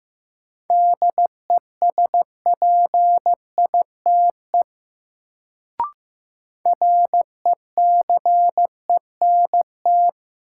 Morse code, heard as a keyed tone, about 15 wpm, 700 Hz. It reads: DESPITE RECENT
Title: Sentences from Top 1000 Words - Review Part 7 - New Word Drill - 15wpm